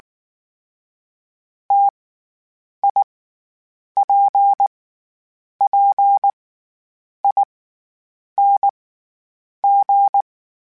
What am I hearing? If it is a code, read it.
Morse code, heard as a keyed tone, about 19 wpm, spaced as for 8 wpm, 800 Hz. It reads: TIPPING
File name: Variable Speed / Variable Speed-mp3s / T AOEI's sp @18